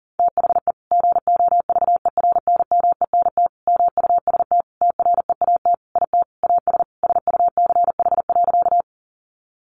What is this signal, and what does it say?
Morse code, heard as a keyed tone, about 40 wpm, 700 Hz. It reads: THE GOVERNMENT MUST TREAT IT AS SUCH.
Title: Sentences from Top 1300 Words 40wpm